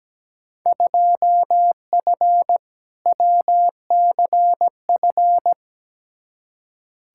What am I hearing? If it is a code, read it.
Morse code, heard as a keyed tone, about 17 wpm, 700 Hz. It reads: 2F WCF
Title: Field Day Exchanges Encore - 17wpm